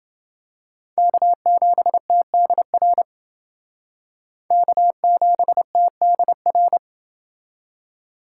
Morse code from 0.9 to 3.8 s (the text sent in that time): K7TDR